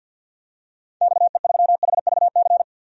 Morse code, heard as a keyed tone, about 50 wpm, 700 Hz. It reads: XE3HVC
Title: DXCC-100 Call Signs - 6 Characters - 50wpm